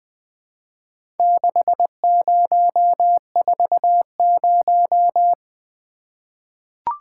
Morse code, heard as a keyed tone, about 20 wpm, 700 Hz. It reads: 6040
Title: Four Letter-Number Combinations - 20wpm